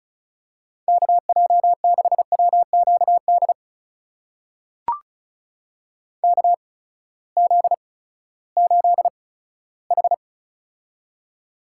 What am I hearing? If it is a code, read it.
Morse code, heard as a keyed tone, about 36 wpm, spaced as for 18 wpm, 700 Hz. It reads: KJ6WQD KZ8H
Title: USA Call Signs - 8x Character Spacing - 35wpm